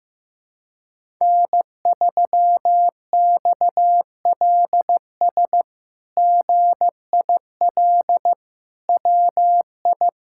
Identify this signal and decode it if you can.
Morse code, heard as a keyed tone, about 15 wpm, 700 Hz. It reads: N3XLS GIL WI
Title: K1USN SST QSO Practice - 15wpm